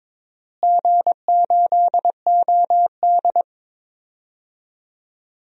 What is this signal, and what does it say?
Morse code, heard as a keyed tone, about 22 wpm, 700 Hz. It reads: G8OD
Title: DXCC-100 Call Signs - 4 Characters - Repeated 3x - 22wpm